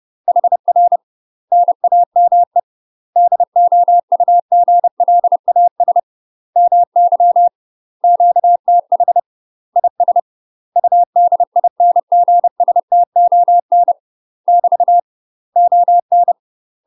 Morse code, heard as a keyed tone, about 30 wpm, 700 Hz. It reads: HR NAME DOUGLAS MY QTH IS UDINGSTON <BT> ON
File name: UK MASTER QSOs TRAINING / 30WPM / UK MASTER QSOs 0039